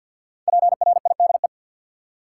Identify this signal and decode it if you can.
Morse code, heard as a keyed tone, about 53 wpm, 700 Hz. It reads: PRIDE
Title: Five Letter Words - 50wpm